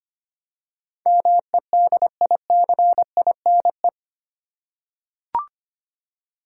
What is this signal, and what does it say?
Morse code, heard as a keyed tone, about 25 wpm, 700 Hz. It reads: MEDICINE
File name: Eight Letter Words - 25wpm